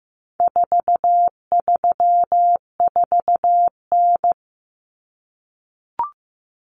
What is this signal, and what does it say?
Morse code, heard as a keyed tone, about 15 wpm, 700 Hz. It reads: 434N
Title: Common Cut Numbers - 4-Digit 15wpm